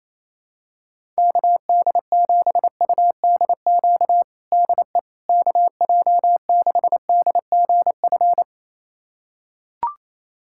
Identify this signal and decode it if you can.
Morse code, heard as a keyed tone, about 28 wpm, 700 Hz. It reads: KD7UDQ DE KJ6DGF
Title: USA Callsign DE Callsign - 6 Characters - 28wpm